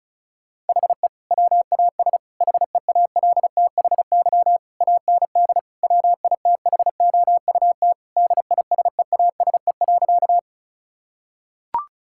Morse code, heard as a keyed tone, about 36 wpm, 700 Hz. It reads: HE WAS HEALTHY AND WITHOUT DISEASE.
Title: Sentences from Top 1000 Words - Review Part 6 - 35wpm